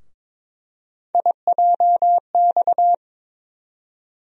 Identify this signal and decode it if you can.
Morse code, heard as a keyed tone, about 22 wpm, 700 Hz. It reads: IJX
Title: USA Call Sign Postfixes - 3 Letters 22wpm